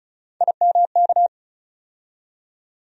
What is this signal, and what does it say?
Morse code, heard as a keyed tone, about 34 wpm, 700 Hz. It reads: IMK
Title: USA Call Sign Postfixes - 3 Letters 35wpm